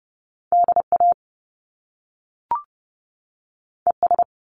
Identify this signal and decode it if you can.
Morse code, heard as a keyed tone, about 30 wpm, 700 Hz. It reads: DA ES
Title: Top Two Letter Combinations in Words 30wpm